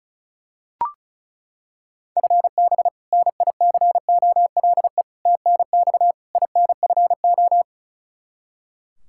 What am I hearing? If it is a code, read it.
Morse code, heard as a keyed tone, about 34 wpm, 700 Hz. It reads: FB NICOLE TNX INFO
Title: Practice QSO - Exchange 3 - More Info - 35wpm